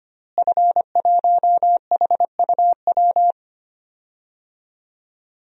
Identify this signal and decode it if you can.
Morse code, heard as a keyed tone, about 25 wpm, 700 Hz. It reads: F1HUW
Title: DXCC-100 Call Signs - 5 Characters - Repeated 3x - 25wpm